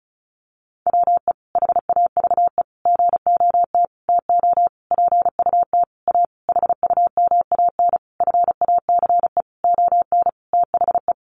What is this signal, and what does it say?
Morse code, heard as a keyed tone, about 34 wpm, 700 Hz. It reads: WE HAVE GOT TO PUT A HUMAN FACE ON THE
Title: Sentences from Top 1200 Words 35wpm